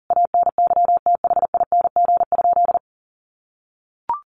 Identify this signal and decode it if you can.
Morse code, heard as a keyed tone, about 40 wpm, 700 Hz. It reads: ANYTHING?
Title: Sentences from Top 1000 Words 40wpm